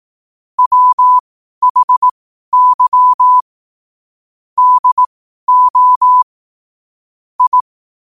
Morse code, heard as a keyed tone, about 18 wpm, spaced as for 12 wpm, 1000 Hz. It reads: WHY DO I
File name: qotdfiles / 12wpm